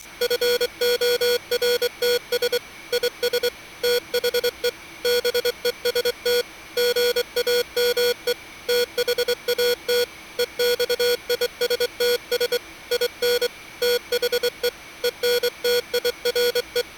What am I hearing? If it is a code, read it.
Morse code, heard as a keyed tone, about 23 wpm, 480 Hz. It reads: FORTS IS THE BEST GAME THAT EXISTS IN THE ENTIRE